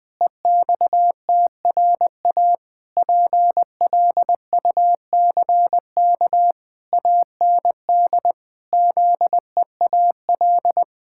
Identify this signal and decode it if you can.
Morse code, heard as a keyed tone, about 20 wpm, 700 Hz. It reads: EXTRA PLUCK AND ZEAL
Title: Pangram Challenge v2 - 20wpm